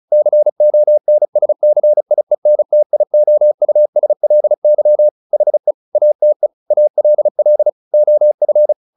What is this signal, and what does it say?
Morse code, heard as a keyed tone, about 34 wpm, 600 Hz. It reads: CONSCIENTIOUSLY HE ATE ALL OF
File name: the old man and the sea / the old man and the sea 1 35wpm